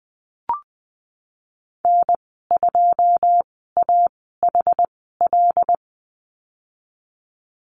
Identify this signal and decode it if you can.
Morse code, heard as a keyed tone, about 20 wpm, spaced as for 15 wpm, 700 Hz. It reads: N2AHL
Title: 161 Lesson 26g - Letter g - Call Signs 20wpm